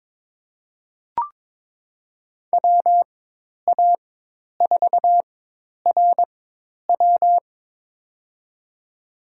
Morse code, heard as a keyed tone, about 22 wpm, spaced as for 10 wpm, 700 Hz. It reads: WA4RW
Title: USA Call Signs - 5 Characters - 4x Character Spacing - 22wpm